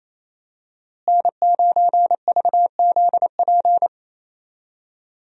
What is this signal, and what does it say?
Morse code, heard as a keyed tone, about 28 wpm, 700 Hz. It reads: N9VZP
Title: USA Call Signs - 5 Characters - Repeated 2x - 28wpm